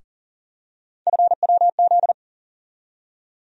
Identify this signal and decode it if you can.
Morse code, heard as a keyed tone, about 40 wpm, 700 Hz. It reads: FWZ
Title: USA Call Sign Postfixes - 3 Letters 40wpm